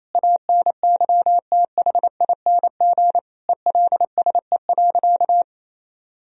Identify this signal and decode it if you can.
Morse code, heard as a keyed tone, about 28 wpm, 700 Hz. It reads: ANYTHING ELSE.
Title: Sentences from Top 1000 Words - Review Part 1 - 28wpm